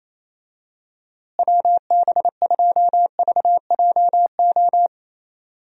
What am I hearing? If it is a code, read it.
Morse code, heard as a keyed tone, about 28 wpm, 700 Hz. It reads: WB2VJO
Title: USA Call Signs - 6 Characters - Repeated 2x - 28wpm